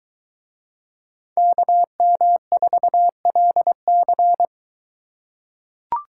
Morse code, heard as a keyed tone, about 21 wpm, 700 Hz. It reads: KM4LC